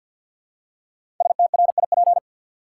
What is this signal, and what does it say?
Morse code, heard as a keyed tone, about 52 wpm, 700 Hz. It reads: STRIP
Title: Five Letter Words - 50wpm